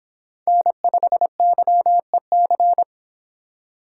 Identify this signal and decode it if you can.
Morse code, heard as a keyed tone, about 24 wpm, 700 Hz. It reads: N5YEC